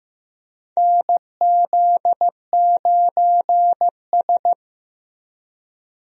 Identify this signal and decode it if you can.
Morse code, heard as a keyed tone, about 16 wpm, 700 Hz. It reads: NZ9S
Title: USA Call Signs - 4 Characters - Repeated 3x - 15wpm